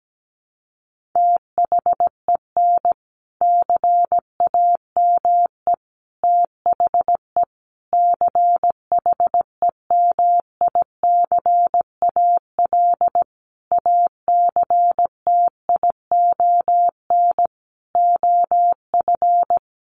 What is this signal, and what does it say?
Morse code, heard as a keyed tone, about 17 wpm, 700 Hz. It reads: THEN CAME THE CHEMICAL ACTION OF
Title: Sentences from Top 1200 Words 17wpm